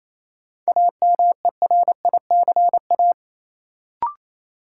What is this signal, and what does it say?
Morse code, heard as a keyed tone, about 28 wpm, 700 Hz. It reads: AMERICA
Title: Sets of 5 Words - New Word Drill - 28wpm